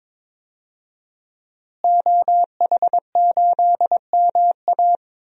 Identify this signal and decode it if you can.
Morse code, heard as a keyed tone, about 21 wpm, 700 Hz. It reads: OH8MA
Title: DXCC-100 Call Signs - 5 Characters - Repeated 3x - 22wpm